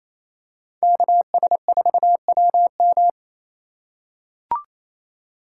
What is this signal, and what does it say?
Morse code, heard as a keyed tone, about 28 wpm, 700 Hz. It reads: KS4WM
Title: USA Call Signs - 2x Character Spacing - 28wpm